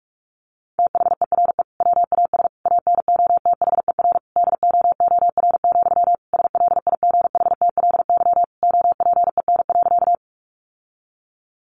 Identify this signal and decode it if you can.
Morse code, heard as a keyed tone, about 44 wpm, 700 Hz. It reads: THERE WAS ANOTHER DOOR, SLIGHTLY OPEN.
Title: Sentences from Top 1300 Words 45wpm